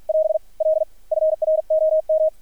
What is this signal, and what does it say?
Morse code, heard as a keyed tone, about 53 wpm, 630 Hz. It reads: <AS> L FROM
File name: archived / CRSnet